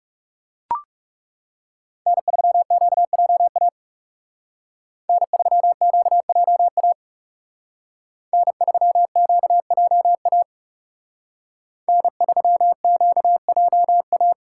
Morse code, heard as a keyed tone, about 36 wpm, 700 Hz. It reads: N3QJA N3QJA N3QJA N3QJA
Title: Speed Racing - USA 1x3 Call Signs 30wpm